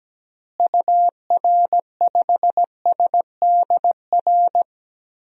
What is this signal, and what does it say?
Morse code, heard as a keyed tone, about 17 wpm, 700 Hz. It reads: UR5SDR
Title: DXCC-100 Call Signs - 6 Characters - Repeated 3x - 17wpm